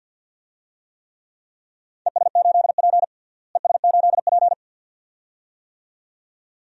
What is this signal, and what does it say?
Morse code, heard as a keyed tone, about 50 wpm, 700 Hz. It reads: ES8P ES8P
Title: DXCC-100 Call Signs - 4 Characters - Repeated 2x - 50wpm